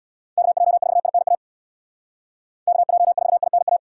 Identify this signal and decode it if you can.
Morse code, heard as a keyed tone, about 79 wpm, 700 Hz. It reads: KJ4INA KJ4INA